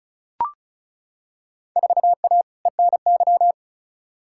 Morse code, heard as a keyed tone, about 36 wpm, 700 Hz. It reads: 4A ENY
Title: Field Day Exchanges - 35wpm